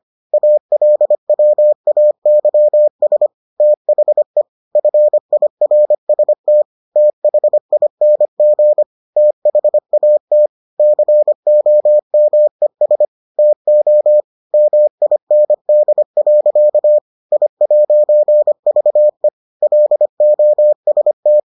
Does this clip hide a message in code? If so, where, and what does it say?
Yes: Morse code from 0.3 to 21.4 s, heard as a keyed tone, about 25 wpm, 600 Hz. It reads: ALWAYS THE FIRST THING THAT COMES TO MIND. I'VE LOST